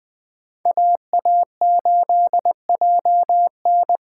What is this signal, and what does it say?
Morse code, heard as a keyed tone, about 20 wpm, 700 Hz. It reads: AA8JN
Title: USA Call Signs - 5 Characters - 8x Character Spacing - 20wpm